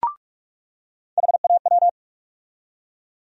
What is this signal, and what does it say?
Morse code, heard as a keyed tone, about 46 wpm, 700 Hz. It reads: HAW
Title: USA Call Sign Postfixes - 3 Letters 45wpm